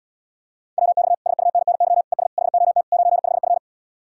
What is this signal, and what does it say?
Morse code, heard as a keyed tone, about 79 wpm, 700 Hz. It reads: WX SUNNY ES HOT 95F